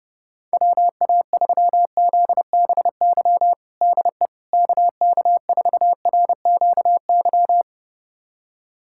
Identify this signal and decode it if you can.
Morse code, heard as a keyed tone, about 32 wpm, 700 Hz. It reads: WA3ZBY DE KK4RQY
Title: USA Callsign DE Callsign 30wpm